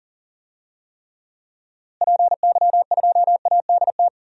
Morse code, heard as a keyed tone, about 40 wpm, 700 Hz. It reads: PY2ADT
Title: DXCC-100 Call Signs - 6 Characters - Repeated 2x - 40wpm